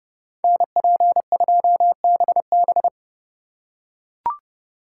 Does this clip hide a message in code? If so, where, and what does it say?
Yes: Morse code from 0.4 to 2.9 s, heard as a keyed tone, about 29 wpm, 700 Hz. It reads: NP2BB